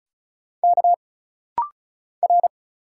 Morse code, heard as a keyed tone, about 35 wpm, 700 Hz. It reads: K R
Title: Single Letter-Number - Rapid-Fire 35wpm